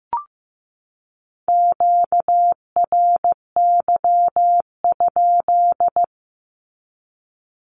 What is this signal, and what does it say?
Morse code, heard as a keyed tone, about 15 wpm, 700 Hz. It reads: QRY?
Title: Q-Codes 15wpm